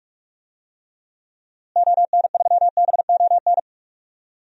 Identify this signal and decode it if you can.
Morse code, heard as a keyed tone, about 46 wpm, 700 Hz. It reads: ON3BON
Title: DXCC-100 Call Signs - 6 Characters - Repeated 2x - 45wpm